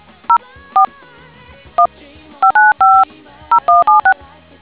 DTMF, heard as telephone touch tones, tones of about 110 ms. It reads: 0 4 1 595 0176